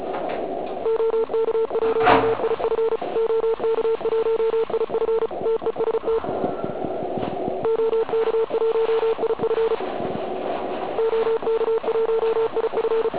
Morse code, heard as a keyed tone, about 34 wpm, 450 Hz, with noise. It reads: OK1IF OK1IF TEST OK1IF OK1IF
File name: Archiv / CUC106 / cuc106 PILEUP